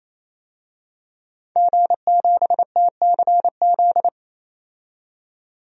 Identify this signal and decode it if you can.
Morse code, heard as a keyed tone, about 28 wpm, 700 Hz. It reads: G7TCZ
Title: DXCC-100 Call Signs - 5 Characters - 28wpm